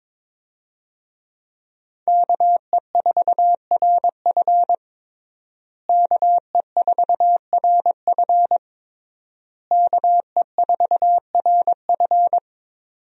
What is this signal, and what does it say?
Morse code, heard as a keyed tone, about 21 wpm, 700 Hz. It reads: KE4RF KE4RF KE4RF
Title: USA Call Signs - 5 Characters - Repeated 3x - 22wpm